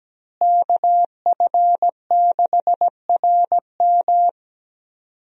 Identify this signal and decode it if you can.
Morse code, heard as a keyed tone, about 17 wpm, 700 Hz. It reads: KF6RM
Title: USA Call Signs - 5 Characters - Repeated 3x - 17wpm